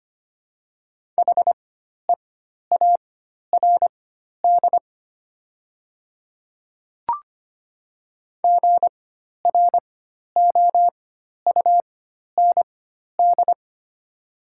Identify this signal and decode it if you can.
Morse code, heard as a keyed tone, about 25 wpm, spaced as for 12 wpm, 700 Hz. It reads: HEARD GROUND
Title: 4x Character Spacing - Top 500 Words - 25wpm